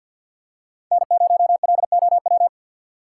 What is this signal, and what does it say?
Morse code, heard as a keyed tone, about 53 wpm, 700 Hz. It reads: N0LOW
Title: USA Call Signs - 5 Characters - 2x Character Spacing - 50wpm